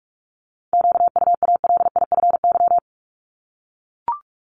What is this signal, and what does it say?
Morse code, heard as a keyed tone, about 44 wpm, 700 Hz. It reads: QUALIFY
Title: Top 1751-2000 Words 45wpm